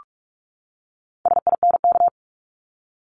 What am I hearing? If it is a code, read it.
Morse code, heard as a keyed tone, about 44 wpm, 700 Hz. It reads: SINK
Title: Top 1751-2000 Words 45wpm